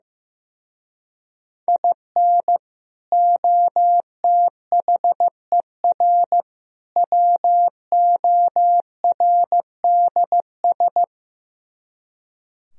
Morse code, heard as a keyed tone, about 15 wpm, 700 Hz. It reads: IN OTHER WORDS
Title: Speed-Racing - Sets of 3 Words 15wpm